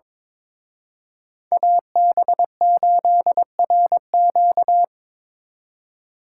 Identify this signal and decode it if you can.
Morse code, heard as a keyed tone, about 25 wpm, 700 Hz. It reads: AB8RQ
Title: Speed Racing - USA Call Signs 22wpm